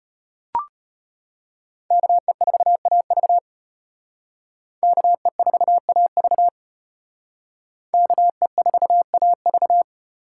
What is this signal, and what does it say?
Morse code, heard as a keyed tone, about 32 wpm, 700 Hz. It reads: KE4AV KE4AV KE4AV